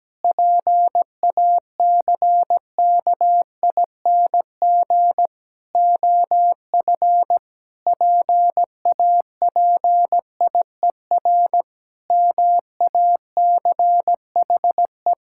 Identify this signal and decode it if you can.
Morse code, heard as a keyed tone, about 17 wpm, 700 Hz. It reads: PACKING OF PAPIER MACHE
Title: Pangram Challenge v2 - 17wpm